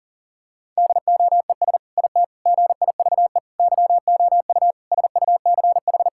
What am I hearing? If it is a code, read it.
Morse code, heard as a keyed tone, about 40 wpm, 700 Hz. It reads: DOES IT GIVE YOU SUCH
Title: Sentences from Top 1600 Words 40wpm